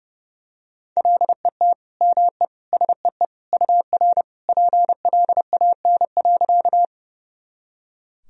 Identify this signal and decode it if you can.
Morse code, heard as a keyed tone, about 32 wpm, 700 Hz. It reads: LET ME SEE UR PLAN.
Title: Sentences from Top 500 Words and Abbreviations 30wpm